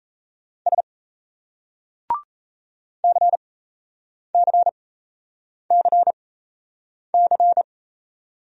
Morse code, heard as a keyed tone, about 38 wpm, spaced as for 8 wpm, 700 Hz. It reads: S CCCC